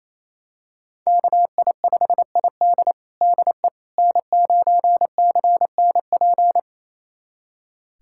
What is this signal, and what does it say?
Morse code, heard as a keyed tone, about 28 wpm, 700 Hz. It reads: KI5ID DE N9CNP
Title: USA Callsign DE Callsign - 5 Characters - 28wpm